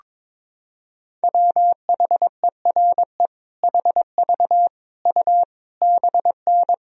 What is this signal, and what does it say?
Morse code, heard as a keyed tone, about 22 wpm, 700 Hz. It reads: WHERE HV U BN
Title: Sentences from Top 300 Words and Abbreviations 22wpm